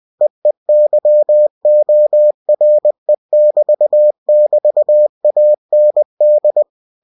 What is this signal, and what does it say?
Morse code, heard as a keyed tone, about 20 wpm, 600 Hz. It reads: EEYORE<BT><BT>AND